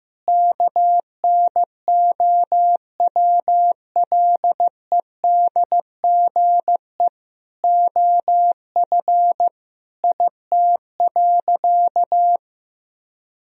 Morse code, heard as a keyed tone, about 16 wpm, 700 Hz. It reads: KNOWLEDGE OF IT.
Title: Sentences from Top 1000 Words - Review Part 3 - 15wpm